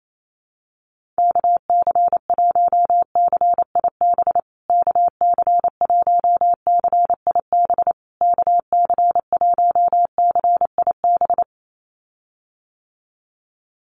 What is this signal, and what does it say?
Morse code, heard as a keyed tone, about 28 wpm, 700 Hz. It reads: KC1CIB KC1CIB KC1CIB
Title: USA Call Signs Repeated 3x 28wpm